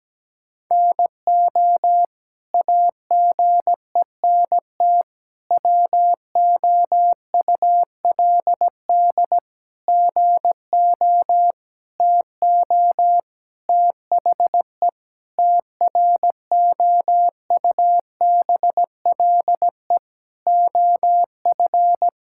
Morse code, heard as a keyed tone, about 17 wpm, 700 Hz. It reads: NO AGENT WOULD GO TO THE TROUBLE OF
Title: Sentences from Top 1600 Words 17wpm